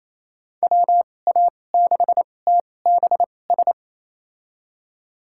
Morse code, heard as a keyed tone, about 28 wpm, spaced as for 20 wpm, 700 Hz. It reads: WA6TBS
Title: USA Call Signs - 6 Characters - 2x Character Spacing - 28wpm